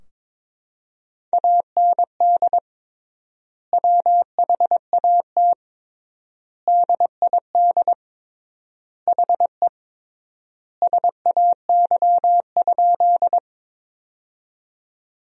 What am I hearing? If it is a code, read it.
Morse code, heard as a keyed tone, about 22 wpm, 700 Hz. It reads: AND WHAT DID HE SAY?
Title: 3x Word-Spacing - Sentences from Top 200 Words - 22wpm